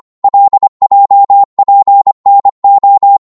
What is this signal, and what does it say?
Morse code, heard as a keyed tone, about 25 wpm, 800 Hz. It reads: LJPNO
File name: Morse / ltr-25wpm